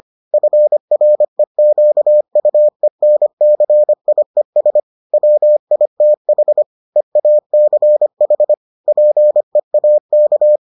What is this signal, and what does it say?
Morse code, heard as a keyed tone, about 25 wpm, 600 Hz. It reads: FREQUENCIES WITH EACH PEAK